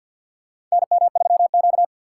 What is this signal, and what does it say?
Morse code, heard as a keyed tone, about 50 wpm, 700 Hz. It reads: NM3Q
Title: USA Call Signs - 2x Character Spacing - 50wpm